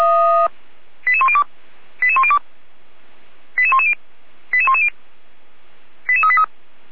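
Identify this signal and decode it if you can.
Five-tone selective call (ZVEI), heard as quick runs of five tones, tones of about 70 ms. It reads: 88102 88102 88110 88110 88383
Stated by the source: ZVEI five-tone call received on an SDR.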